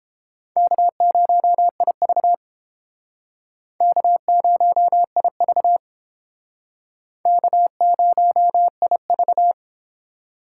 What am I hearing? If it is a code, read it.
Morse code, heard as a keyed tone, about 31 wpm, 700 Hz. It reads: K0IV K0IV K0IV